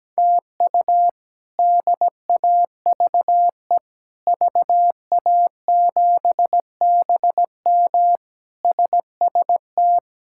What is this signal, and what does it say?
Morse code, heard as a keyed tone, about 17 wpm, 700 Hz. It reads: TU DAVE VA7BM SST
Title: K1USN SST QSO Practice - 17wpm